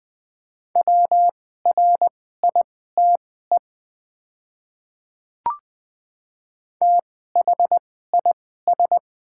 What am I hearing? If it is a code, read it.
Morse code, heard as a keyed tone, about 20 wpm, spaced as for 14 wpm, 700 Hz. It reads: WRITE THIS
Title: 2x Character Spacing - Top 100 Words - 20wpm